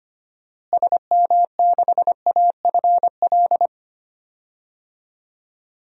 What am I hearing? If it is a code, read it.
Morse code, heard as a keyed tone, about 25 wpm, 700 Hz. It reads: SM6AFL
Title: DXCC-100 Call Signs - 6 Characters - Repeated 3x - 25wpm